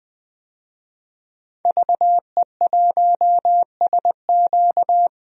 Morse code, heard as a keyed tone, about 20 wpm, 700 Hz. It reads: VE1SQ
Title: DXCC-100 Call Signs - 5 Characters - 20wpm